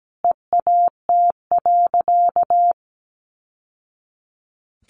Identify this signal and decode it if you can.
Morse code, heard as a keyed tone, about 17 wpm, 700 Hz. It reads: EAT.
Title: Sentences from Top 700 Words 17wpm